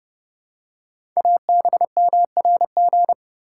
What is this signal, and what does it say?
Morse code, heard as a keyed tone, about 30 wpm, 700 Hz. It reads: ABMRG
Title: Code Groups - 5 Letters - 30wpm